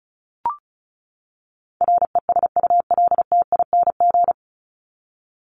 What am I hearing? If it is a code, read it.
Morse code, heard as a keyed tone, about 34 wpm, 700 Hz. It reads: RESULTING
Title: ING Suffix - Top 100 Words with Base Words 35wpm